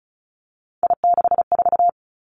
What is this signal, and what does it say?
Morse code, heard as a keyed tone, about 36 wpm, 700 Hz. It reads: I64
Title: Three Challenging Letter-Number Combinations 35wpm